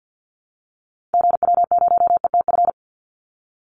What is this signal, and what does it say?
Morse code, heard as a keyed tone, about 50 wpm, 700 Hz. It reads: GW0ETF
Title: Call Signs - 50wpm